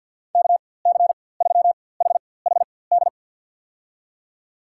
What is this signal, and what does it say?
Morse code, heard as a keyed tone, about 51 wpm, spaced as for 23 wpm, 700 Hz. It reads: KC3HHD
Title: USA Call Signs - 4x Character Spacing - 50wpm